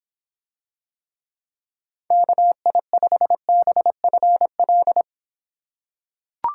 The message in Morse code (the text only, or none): KI5BFL